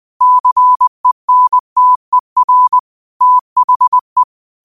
Morse code, heard as a keyed tone, about 20 wpm, 1000 Hz. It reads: CENTER THE
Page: New quotes every day in morse code at 20 Words per minute.